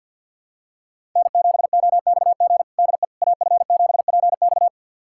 Morse code, heard as a keyed tone, about 52 wpm, 700 Hz. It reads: N7OXG DE AF7PX
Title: USA Callsign DE Callsign - 5 Characters - 50wpm